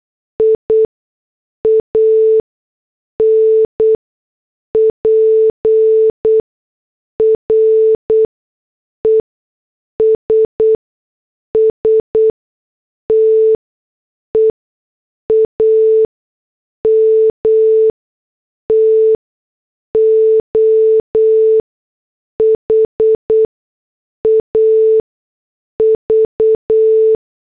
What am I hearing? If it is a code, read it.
Morse code, heard as a keyed tone, about 8 wpm, spaced as for 6 wpm, 440 Hz. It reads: IAN PRESS TEAM TO HAV
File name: morse / headlines